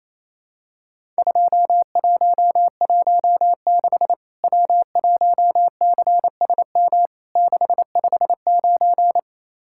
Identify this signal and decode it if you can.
Morse code, heard as a keyed tone, about 28 wpm, 700 Hz. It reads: 2116 W1CSM 659
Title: Worked All Europe DX Contest - QTC Exchange - 28wpm